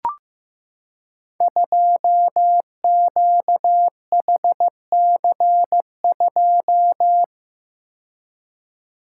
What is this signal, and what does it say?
Morse code, heard as a keyed tone, about 15 wpm, 700 Hz. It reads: 2QHC2
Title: Code Groups - 5 Letter:Numbers - 15wpm